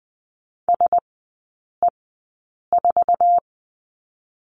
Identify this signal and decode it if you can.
Morse code, heard as a keyed tone, about 20 wpm, spaced as for 8 wpm, 700 Hz. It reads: SE4
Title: 053 Lesson 10e - Letter r - Three Characters 20wpm